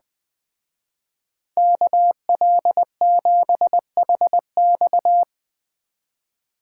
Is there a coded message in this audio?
Yes: Morse code from 1.6 to 5.2 s, heard as a keyed tone, about 20 wpm, 700 Hz. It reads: KL7HX